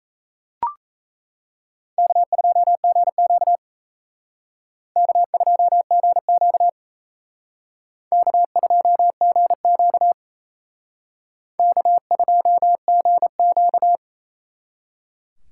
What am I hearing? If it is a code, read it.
Morse code, heard as a keyed tone, about 35 wpm, 700 Hz. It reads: K2GQ K2GQ K2GQ K2GQ